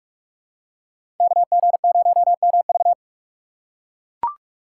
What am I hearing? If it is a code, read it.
Morse code, heard as a keyed tone, about 44 wpm, 700 Hz. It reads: KG0MV